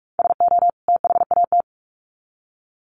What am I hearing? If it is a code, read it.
Morse code, heard as a keyed tone, about 44 wpm, 700 Hz. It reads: SO THAT
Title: Sets of 2 Words 45wpm